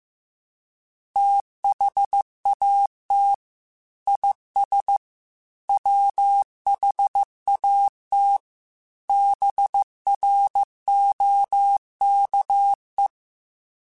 Morse code, heard as a keyed tone, about 15 wpm, 800 Hz. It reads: THAT IS WHAT BROKE